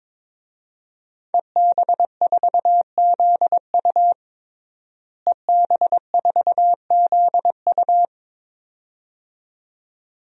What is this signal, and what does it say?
Morse code, heard as a keyed tone, about 21 wpm, 700 Hz. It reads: EB4ZU EB4ZU
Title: DXCC-100 Call Signs - 5 Characters - Repeated 2x - 22wpm